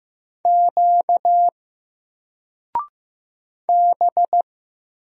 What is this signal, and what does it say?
Morse code, heard as a keyed tone, about 15 wpm, 700 Hz. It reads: Q B
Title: Single Character (Letter:Number:Punctuation:Prosign) Alt 15wpm